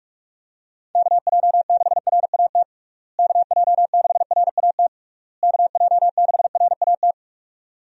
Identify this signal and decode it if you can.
Morse code, heard as a keyed tone, about 41 wpm, 700 Hz. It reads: KJ6RAT KJ6RAT KJ6RAT
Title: USA Call Signs - 6 Characters - Repeated 3x - 45wpm